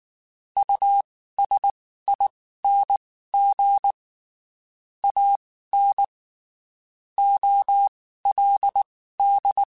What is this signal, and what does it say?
Morse code, heard as a keyed tone, about 19 wpm, spaced as for 14 wpm, 800 Hz. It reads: USING AN OLD
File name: QSOs / QSOs-mp3s / WD0FBT 1@18